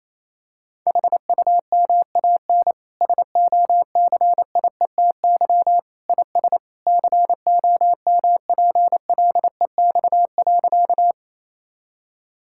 Morse code, heard as a keyed tone, about 28 wpm, 700 Hz. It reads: HUMAN SOCIETY IS COMPLEX.
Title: Sentences from Top 1000 Words - Review Part 6 - 28wpm